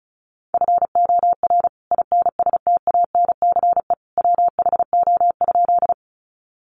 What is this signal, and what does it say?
Morse code, heard as a keyed tone, about 34 wpm, 700 Hz. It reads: FOR INSTANCE WHO?
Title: Sentences from Top 800 Words 35wpm